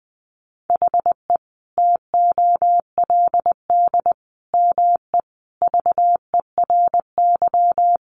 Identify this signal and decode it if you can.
Morse code, heard as a keyed tone, about 20 wpm, 700 Hz. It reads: HE TOLD ME VERY
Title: Sentences from Top 1500 Words 20wpm